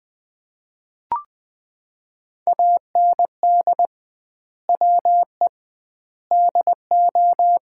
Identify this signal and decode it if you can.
Morse code, heard as a keyed tone, about 20 wpm, 700 Hz. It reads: AND WE DO
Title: 2x Word-Spacing - Sets of 4 Words - 20wpm